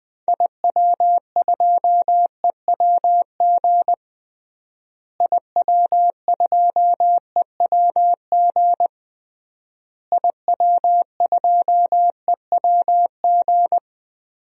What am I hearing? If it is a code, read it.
Morse code, heard as a keyed tone, about 20 wpm, 700 Hz. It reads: IW2EWG IW2EWG IW2EWG
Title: DXCC-100 Call Signs - 6 Characters - Repeated 3x - 20wpm